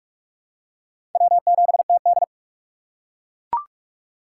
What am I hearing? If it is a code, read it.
Morse code, heard as a keyed tone, about 43 wpm, 700 Hz. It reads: W7TD